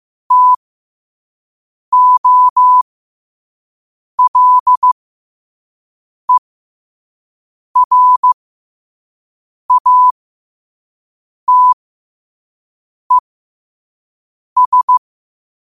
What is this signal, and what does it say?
Morse code, heard as a keyed tone, about 15 wpm, spaced as for 5 wpm, 1000 Hz. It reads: TOLERATES
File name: qotdfiles / 5wpm